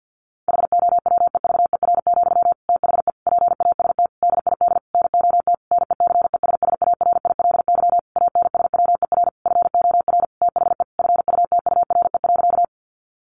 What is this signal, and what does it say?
Morse code, heard as a keyed tone, about 50 wpm, 700 Hz. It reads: HOWEVER, THE PAST DID NOT NECESSARILY ANSWER FOR THE FUTURE.
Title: Sentences from Top 1400 Words 50wpm